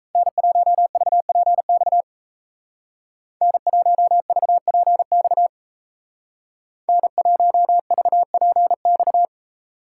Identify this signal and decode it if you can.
Morse code, heard as a keyed tone, about 36 wpm, 700 Hz. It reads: N1VPX N1VPX N1VPX